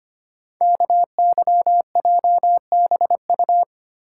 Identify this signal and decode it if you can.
Morse code, heard as a keyed tone, about 25 wpm, 700 Hz. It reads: KYJBU
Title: Code Groups - 5 Letters - 25wpm